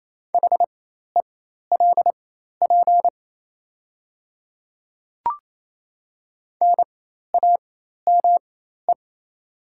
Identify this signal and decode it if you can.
Morse code, heard as a keyed tone, about 28 wpm, spaced as for 13 wpm, 700 Hz. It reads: HELP NAME
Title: 4x Character Spacing - Top 200 Words - 28wpm